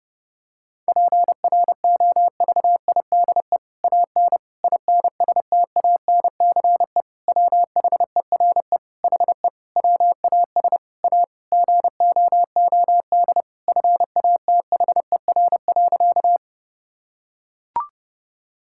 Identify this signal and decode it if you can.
Morse code, heard as a keyed tone, about 30 wpm, 700 Hz. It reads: PROVIDE AN INSTANCE WHERE HE WAS A GOOD FATHER.
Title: Sentences from Top 1000 Words - Review Part 5 - 30wpm